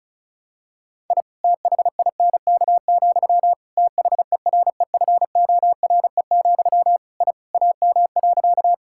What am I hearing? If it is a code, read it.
Morse code, heard as a keyed tone, about 34 wpm, 700 Hz. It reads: I THINK, THEREFORE, I AM.
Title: Sentences from Top 1000 Words - Review Part 5 - 35wpm